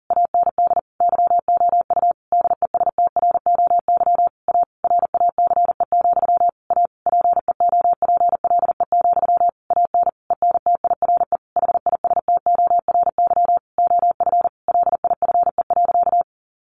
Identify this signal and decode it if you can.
Morse code, heard as a keyed tone, about 40 wpm, 700 Hz. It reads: AND YOU DESTROY A RACE, A PEOPLE, AN ENTIRE HISTORY OF LIFE.
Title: A Sound Of Thunder 40wpm